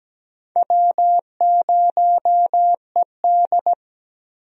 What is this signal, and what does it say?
Morse code, heard as a keyed tone, about 17 wpm, 700 Hz. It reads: W0ED
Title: USA Call Signs - 4 Characters - Repeated 3x - 17wpm